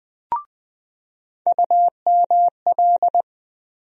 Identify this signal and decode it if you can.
Morse code, heard as a keyed tone, about 21 wpm, 700 Hz. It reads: UML
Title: USA Call Sign Postfixes - 3 Letters 20wpm